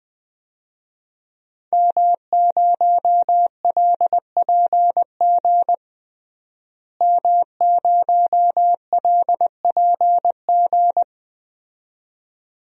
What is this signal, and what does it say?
Morse code, heard as a keyed tone, about 20 wpm, 700 Hz. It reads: M0LPG M0LPG
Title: DXCC-100 Call Signs - 5 Characters - Repeated 2x - 20wpm